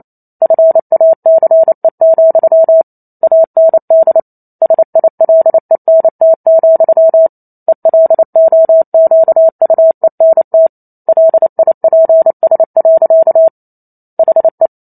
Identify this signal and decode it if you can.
Morse code, heard as a keyed tone, about 28 wpm, 650 Hz. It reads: FACE, AND SILENT, ELOQUENT LIPS. HE